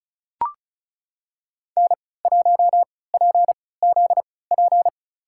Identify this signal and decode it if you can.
Morse code, heard as a keyed tone, about 34 wpm, spaced as for 20 wpm, 700 Hz. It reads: N1PZP
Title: USA Call Signs - 5 Characters - 3x Character Spacing - 35wpm